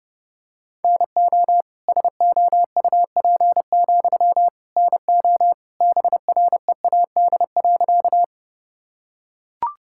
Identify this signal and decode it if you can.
Morse code, heard as a keyed tone, about 32 wpm, 700 Hz. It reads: NO SOUP, NO BREAD.
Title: Sentences from Top 1600 Words 30wpm